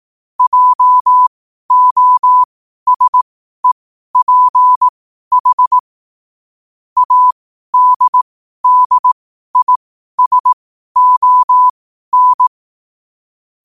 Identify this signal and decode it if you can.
Morse code, heard as a keyed tone, about 18 wpm, spaced as for 12 wpm, 1000 Hz. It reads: JOSEPH ADDISON
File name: 12wpm